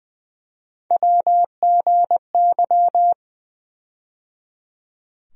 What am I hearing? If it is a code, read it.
Morse code, heard as a keyed tone, about 20 wpm, 700 Hz. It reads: WGY
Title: USA Call Sign Postfixes - 3 Letters 20wpm